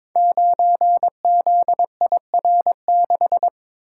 Morse code, heard as a keyed tone, about 23 wpm, 700 Hz. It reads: 9ZIR6
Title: Code Groups - 5 Letter:Numbers - 22wpm